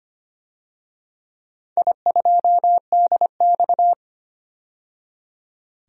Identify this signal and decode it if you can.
Morse code, heard as a keyed tone, about 25 wpm, 700 Hz. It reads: I2DX
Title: DXCC-100 Call Signs - 4 Characters - 25wpm